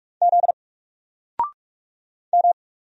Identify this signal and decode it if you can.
Morse code, heard as a keyed tone, about 45 wpm, 700 Hz. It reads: Z M
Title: Single Letter-Number 45wpm